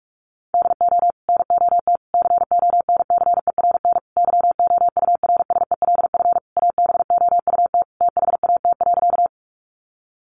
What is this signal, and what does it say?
Morse code, heard as a keyed tone, about 44 wpm, 700 Hz. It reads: DO NOT CONCERN YOURSELF ABOUT THAT.
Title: Sentences from Top 1000 Words 45wpm